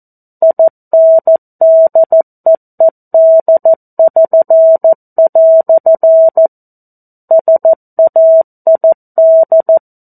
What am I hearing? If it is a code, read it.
Morse code, heard as a keyed tone, about 14 wpm, 650 Hz. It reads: INDEED<SN>" SAID
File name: 02-Chapter-02 14 WPM